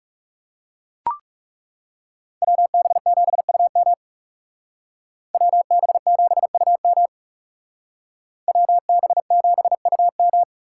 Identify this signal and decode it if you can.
Morse code, heard as a keyed tone, about 34 wpm, 700 Hz. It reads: WB7UM WB7UM WB7UM
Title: Speed Racing - USA 2x2 Call Signs 30wpm